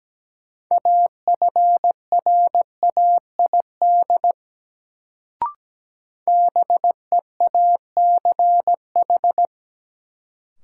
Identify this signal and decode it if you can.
Morse code, heard as a keyed tone, about 17 wpm, 700 Hz. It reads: AFRAID BEACH
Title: Sentences from Top 1500 Words - New Word Drill - 17wpm